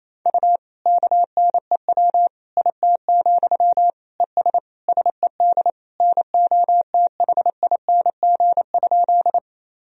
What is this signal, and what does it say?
Morse code, heard as a keyed tone, about 28 wpm, 700 Hz. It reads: U KNEW IT, ES SED NOTHING?
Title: Sentences from Top 500 Words and Abbreviations 28wpm